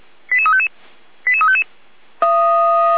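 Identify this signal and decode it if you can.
Five-tone selective call (ZVEI), heard as quick runs of five tones, tones of about 70 ms. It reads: 88355 88355
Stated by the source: Received by SDR; ZVEI five-tone call 88352, channel 455.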